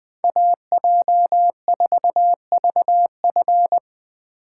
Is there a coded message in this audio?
Yes: Morse code from 0.2 to 3.8 s, heard as a keyed tone, about 20 wpm, 700 Hz. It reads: AJ4VF